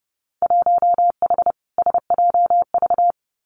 Morse code, heard as a keyed tone, about 30 wpm, 700 Hz. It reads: 1H SJV
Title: Winter Field Day Exchanges 30wpm